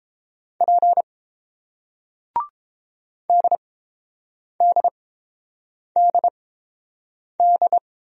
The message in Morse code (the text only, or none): P DDDD